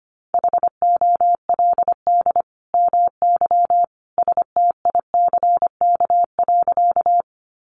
Morse code, heard as a keyed tone, about 25 wpm, 700 Hz. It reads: HOLD MY STICK.
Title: Sentences from Top 1200 Words 25wpm